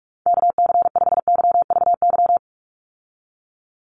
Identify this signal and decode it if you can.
Morse code, heard as a keyed tone, about 44 wpm, 700 Hz. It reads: KC5YVY
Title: USA Call Signs 45wpm